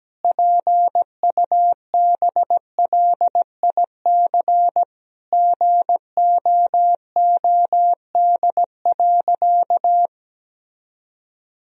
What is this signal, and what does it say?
Morse code, heard as a keyed tone, about 17 wpm, 700 Hz. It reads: PUBLIC GOOD.
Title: Sentences from Top 1000 Words - Review Part 3 - 17wpm